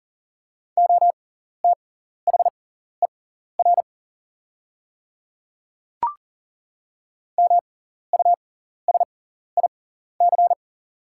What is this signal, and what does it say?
Morse code, heard as a keyed tone, about 40 wpm, spaced as for 14 wpm, 690 Hz. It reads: OTHER MUSIC
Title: 6x Character Spacing - Top 300 Words - 40wpm